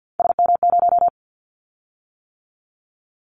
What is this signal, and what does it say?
Morse code, heard as a keyed tone, about 50 wpm, 700 Hz. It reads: SM0
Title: Callsign Prefixes for Top 25 DXCC - 50wpm